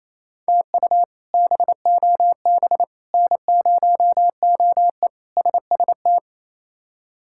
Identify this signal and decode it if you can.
Morse code, heard as a keyed tone, about 28 wpm, 700 Hz. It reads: TU BOB N0OE SST
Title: K1USN SST QSO Practice - 28wpm